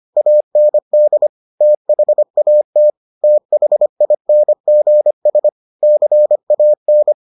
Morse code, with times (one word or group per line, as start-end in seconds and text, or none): 0.2-1.3 AND
1.6-2.9 THAT
3.2-5.5 THINGS
5.8-7.1 CAN